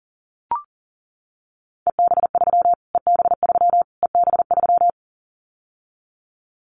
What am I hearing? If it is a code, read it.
Morse code, heard as a keyed tone, about 40 wpm, 700 Hz. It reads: EB3 EB3 EB3
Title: Callsign Prefixes for Top 25 DXCC - Repeated 3x - 40wpm